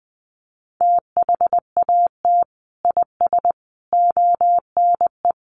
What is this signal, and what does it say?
Morse code, heard as a keyed tone, about 20 wpm, 700 Hz. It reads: THAT IS ONE
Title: Sets of 5 Words Encore 20wpm